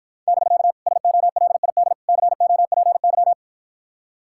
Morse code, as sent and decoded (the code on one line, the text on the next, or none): -...-.- / ... --- .-.. .. -.. / -.-. --- .--. -.--
<BK> SOLID COPY